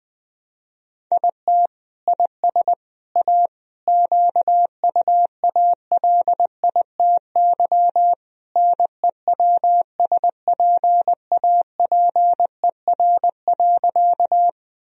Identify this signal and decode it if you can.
Morse code, heard as a keyed tone, about 19 wpm, 700 Hz. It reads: IT IS A QUALITY NEWSPAPER.
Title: Sentences from Top 1000 Words - Review Part 4 - 20wpm